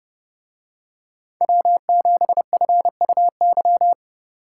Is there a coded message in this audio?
Yes: Morse code from 1.4 to 3.9 s, heard as a keyed tone, about 30 wpm, 700 Hz. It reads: W7FUY